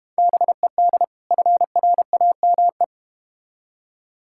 Morse code, heard as a keyed tone, about 31 wpm, 700 Hz. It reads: BED FRAME
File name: practice / 25wpm / Difficult Words - Speed-Racing with Hints - 25wpm